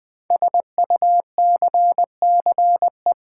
Morse code, heard as a keyed tone, about 20 wpm, 700 Hz. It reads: SUCCE
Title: Build A Word - 5 Letters At A Time - 12-Letter Words - 20wpm